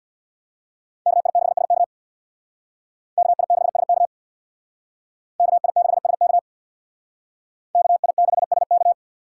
Morse code, heard as a keyed tone, about 62 wpm, 700 Hz. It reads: KI6SK KI6SK KI6SK KI6SK
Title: Speed Racing - USA 2x2 Call Signs 50wpm